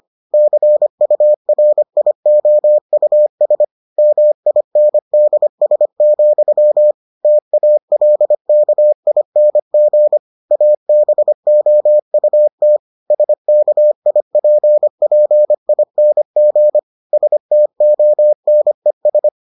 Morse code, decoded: CURIOUS MINDS, TALKING ABOUT SKIPPING STONES

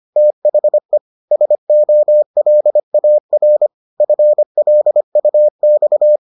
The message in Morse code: THE SOLAR FLUX

